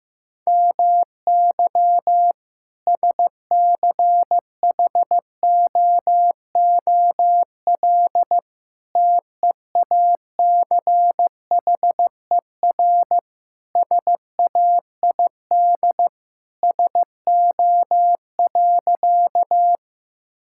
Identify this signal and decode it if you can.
Morse code, heard as a keyed tone, about 15 wpm, 700 Hz. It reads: MY SCHOOL TEACHER SAID SO.
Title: Sentences from Top 1000 Words - Review Part 6 - 15wpm